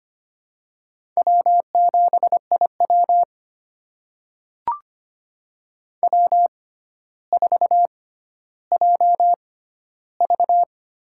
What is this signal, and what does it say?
Morse code, heard as a keyed tone, about 25 wpm, spaced as for 13 wpm, 700 Hz. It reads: W7IW W4JV
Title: USA Call Signs - 4 Characters - 6x Character Spacing - 25wpm